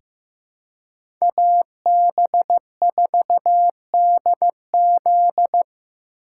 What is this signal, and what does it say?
Morse code, heard as a keyed tone, about 15 wpm, 700 Hz. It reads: AB4DZ
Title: USA Call Signs - 3x Character Spacing - 15wpm